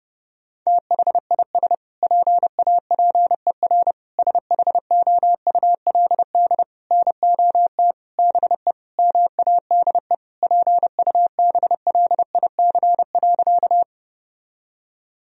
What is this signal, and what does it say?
Morse code, heard as a keyed tone, about 32 wpm, 700 Hz. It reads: THIS PAPER SHOULD NOT BE MADE PUBLIC.
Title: Sentences from Top 1000 Words - Review Part 3 - 30wpm